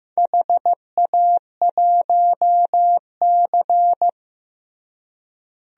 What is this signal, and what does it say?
Morse code, heard as a keyed tone, about 15 wpm, 700 Hz. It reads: HA1C
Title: DXCC-100 Call Signs - 4 Characters - Repeated 3x - 15wpm